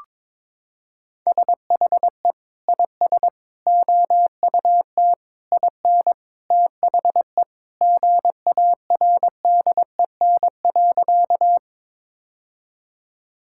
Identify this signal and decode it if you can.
Morse code, heard as a keyed tone, about 22 wpm, 700 Hz. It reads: SHE IS OUT IN THE GARDEN.
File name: practice / 22wpm / Sentences with 3-Word Phrases - 22wpm